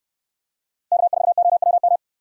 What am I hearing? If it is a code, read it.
Morse code, heard as a keyed tone, about 66 wpm, 700 Hz. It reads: K4OWM